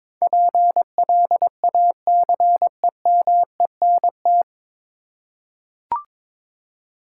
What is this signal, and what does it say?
Morse code, heard as a keyed tone, about 21 wpm, 700 Hz. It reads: PLACEMENT
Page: Morse Code Ninja practice 22wpm